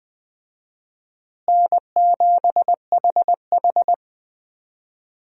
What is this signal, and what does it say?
Morse code, heard as a keyed tone, about 21 wpm, 700 Hz. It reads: N7HH
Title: USA Call Signs - 4 Characters - Repeated 2x - 20wpm